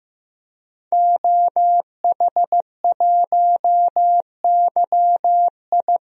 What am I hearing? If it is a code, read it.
Morse code, heard as a keyed tone, about 15 wpm, 700 Hz. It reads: OH1YI
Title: DXCC-100 Call Signs - 5 Characters - Repeated 2x - 15wpm